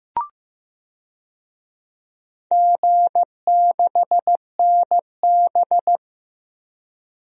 Morse code, heard as a keyed tone, about 16 wpm, 700 Hz. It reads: G6NB
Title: DXCC-100 Call Signs - 4 Characters - Repeated 2x - 15wpm